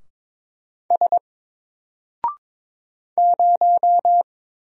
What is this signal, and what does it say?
Morse code, heard as a keyed tone, about 22 wpm, 700 Hz. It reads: S 0
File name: Single-Letter-Number-22wpm